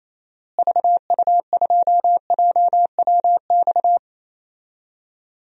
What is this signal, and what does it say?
Morse code, heard as a keyed tone, about 28 wpm, 700 Hz. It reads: VU2JWX
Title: DXCC-100 Call Signs - 6 Characters - Repeated 2x - 28wpm